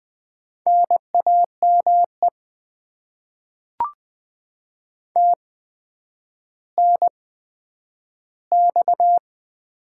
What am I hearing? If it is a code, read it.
Morse code, heard as a keyed tone, about 20 wpm, spaced as for 10 wpm, 700 Hz. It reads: NAME TNX
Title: Top 20 QSO Elements - 8x Character Spacing - 20wpm